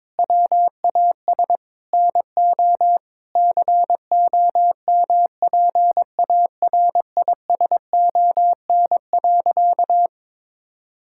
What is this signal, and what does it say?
Morse code, heard as a keyed tone, about 22 wpm, 700 Hz. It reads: WAS NO COMPARISON.
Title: Sentences from Top 1600 Words 22wpm